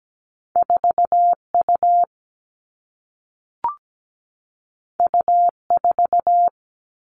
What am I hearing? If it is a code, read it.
Morse code, heard as a keyed tone, about 17 wpm, 700 Hz. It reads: 4 U U 4
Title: Two Challenging Letter-Number Combinations 17wpm